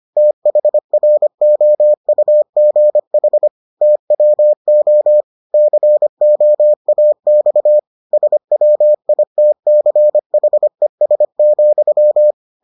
Morse code, heard as a keyed tone, about 25 wpm, 600 Hz. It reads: THROUGH TWO COAX SWITCHES,